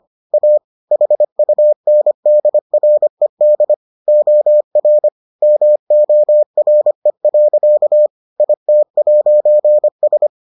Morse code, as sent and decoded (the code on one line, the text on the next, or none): .- / .... ..- -. -.. .-. -.. / --- .-. / -- --- .-. .-.-.- / .. - .----. ...
A HUNDRED OR MORE. IT'S